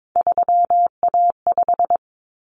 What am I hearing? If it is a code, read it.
Morse code, heard as a keyed tone, about 22 wpm, 700 Hz. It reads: 3A5
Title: Common Cut Numbers - 3-Digit 22wpm